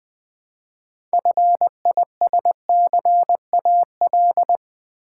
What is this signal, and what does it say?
Morse code, heard as a keyed tone, about 20 wpm, 690 Hz. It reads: FISCAL
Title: Build A Word - 3 Letters At A Time - 6-Letter Words - 20wpm